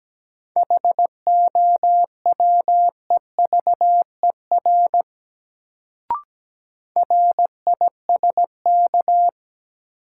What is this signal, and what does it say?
Morse code, heard as a keyed tone, about 17 wpm, 700 Hz. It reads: HOWEVER RISK
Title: Sentences from Top 600 Words - New Word Drill - 17wpm